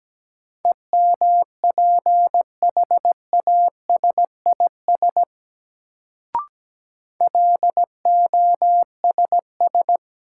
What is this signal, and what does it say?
Morse code, heard as a keyed tone, about 17 wpm, 700 Hz. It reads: EMPHASIS LOSS
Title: Sentences from Top 1400 Words - New Word Drill - 17wpm